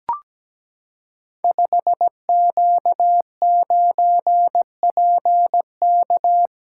Morse code, heard as a keyed tone, about 17 wpm, 700 Hz. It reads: 5Q9PK
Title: Code Groups - 5 Letter:Numbers - 17wpm